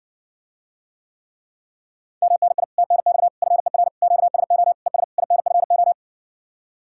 Morse code, heard as a keyed tone, about 69 wpm, 700 Hz. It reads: MNI TNX FER QSO ES INFO